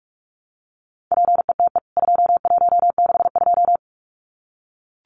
Morse code, heard as a keyed tone, about 44 wpm, 700 Hz. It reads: PETE 2162
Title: CWT Exchange 45wpm